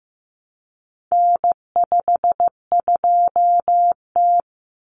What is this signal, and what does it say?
Morse code, heard as a keyed tone, about 15 wpm, 700 Hz. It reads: N52T
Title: Common Cut Numbers - 4-Digit 15wpm